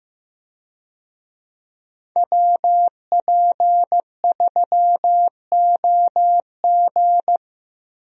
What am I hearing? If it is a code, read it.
Morse code, heard as a keyed tone, about 15 wpm, 700 Hz. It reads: WP3OG
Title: USA Call Signs - 5 Characters - 15wpm